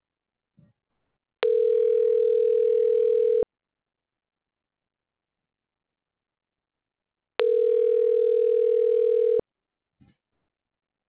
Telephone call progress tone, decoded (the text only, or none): ringback tone